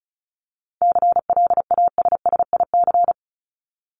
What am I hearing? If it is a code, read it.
Morse code, heard as a keyed tone, about 34 wpm, 700 Hz. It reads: CLASSIC
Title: Top 751-1000 Words 35wpm